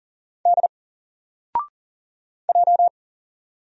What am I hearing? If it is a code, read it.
Morse code, heard as a keyed tone, about 40 wpm, 700 Hz. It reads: D J
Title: Single Letter-Number 40wpm